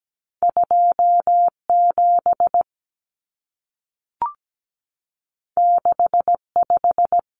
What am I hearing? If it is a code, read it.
Morse code, heard as a keyed tone, about 17 wpm, 700 Hz. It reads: 27 65
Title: Numbers - 2-Digit 17wpm